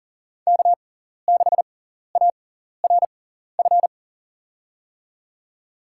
Morse code, heard as a keyed tone, about 40 wpm, spaced as for 14 wpm, 700 Hz. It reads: K6ARF
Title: USA Call Signs - 6x Character Spacing - 40wpm